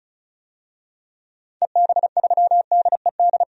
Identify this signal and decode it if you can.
Morse code, heard as a keyed tone, about 34 wpm, 700 Hz. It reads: EB3DED
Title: DXCC-100 Call Signs - 6 Characters - 35wpm